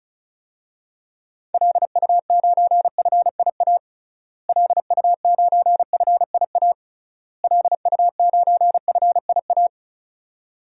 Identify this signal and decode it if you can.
Morse code, heard as a keyed tone, about 34 wpm, 700 Hz. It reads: LU9FIA LU9FIA LU9FIA
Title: DXCC-100 Call Signs - 6 Characters - Repeated 3x - 35wpm